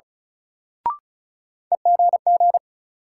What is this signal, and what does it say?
Morse code, heard as a keyed tone, about 34 wpm, 700 Hz. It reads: EGG
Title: Sentences from Top 1000 Words - Review Part 7 - New Word Drill - 35wpm